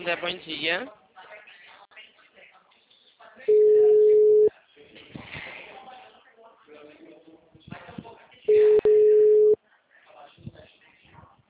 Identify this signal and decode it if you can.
Ringback tone (Europe, 425 Hz), with speech.
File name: out- -2016- - - .4543